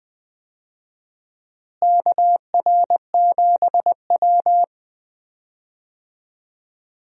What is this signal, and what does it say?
Morse code, heard as a keyed tone, about 20 wpm, 700 Hz. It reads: KR7W
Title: USA Call Signs - 4 Characters - Repeated 3x - 20wpm